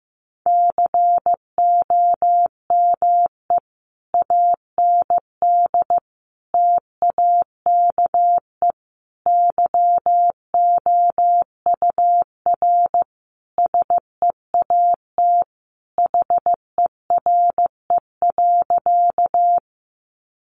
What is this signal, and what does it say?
Morse code, heard as a keyed tone, about 15 wpm, 700 Hz. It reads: COME AND TAKE YOUR SEAT HERE.
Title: Sentences from Top 1500 Words 15wpm